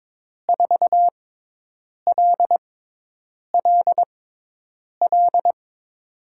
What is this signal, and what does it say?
Morse code, heard as a keyed tone, about 21 wpm, spaced as for 8 wpm, 700 Hz. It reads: 4LLL
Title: USA Call Signs - 5 Characters - 6x Character Spacing - 22wpm